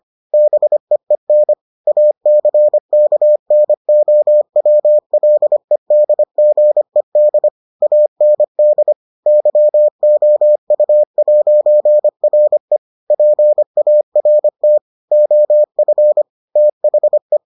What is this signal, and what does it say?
Morse code, heard as a keyed tone, about 25 wpm, 600 Hz. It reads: BEEN ACKNOWLEDGED AND YOU'RE PART OF THE